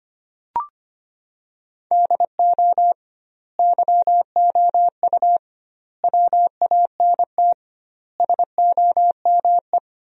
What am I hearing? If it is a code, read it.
Morse code, heard as a keyed tone, about 25 wpm, 700 Hz. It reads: DO YOU WANT SOME
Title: 2x Word-Spacing - Sentences from Top 300 Words - 25wpm